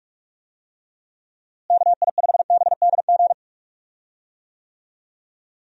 Morse code, heard as a keyed tone, about 48 wpm, 700 Hz. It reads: KI5BDG
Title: USA Call Signs - 6 Characters - Repeated 2x - 45wpm